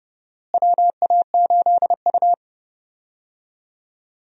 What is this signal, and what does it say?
Morse code, heard as a keyed tone, about 30 wpm, 700 Hz. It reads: WA8U
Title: USA Call Signs - 4 Characters - Repeated 2x - 30wpm